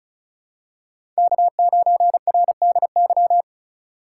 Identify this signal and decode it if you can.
Morse code, heard as a keyed tone, about 34 wpm, 700 Hz. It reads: K9RDY
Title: USA Call Signs - 3x Character Spacing - 35wpm